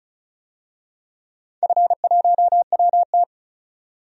Morse code, heard as a keyed tone, about 36 wpm, 700 Hz. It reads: F1WT
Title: DXCC-100 Call Signs - 4 Characters - 35wpm